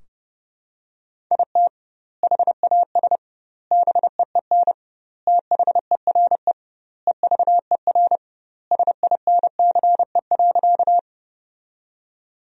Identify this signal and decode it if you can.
Morse code, heard as a keyed tone, about 30 wpm, 700 Hz. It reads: IT HAS BEEN THERE EVER SINCE.
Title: 2x Word-Spacing - Sentences from Top 400 Words - 30wpm